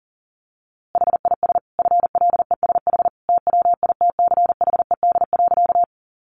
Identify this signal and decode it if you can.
Morse code, heard as a keyed tone, about 40 wpm, 700 Hz. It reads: HIS FLESH TWITCHED.
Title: A Sound Of Thunder 40wpm